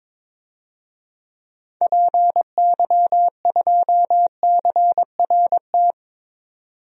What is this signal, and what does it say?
Morse code, heard as a keyed tone, about 23 wpm, 700 Hz. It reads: PY2CRT
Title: DXCC-100 Call Signs - 6 Characters - 22wpm